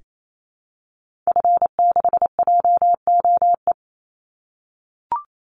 Morse code, heard as a keyed tone, about 28 wpm, 700 Hz. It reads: F6JOE
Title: Call Signs - 28wpm